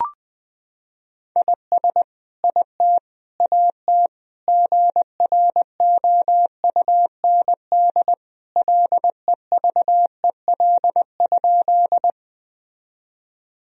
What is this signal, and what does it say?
Morse code, heard as a keyed tone, about 20 wpm, 700 Hz. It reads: IS IT AT GROUND LEVEL?
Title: Sentences from Top 1000 Words - Review Part 4 - 20wpm